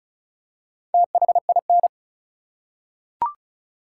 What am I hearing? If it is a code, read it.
Morse code, heard as a keyed tone, about 36 wpm, 700 Hz. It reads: THIN
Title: Four Letter Words - 35wpm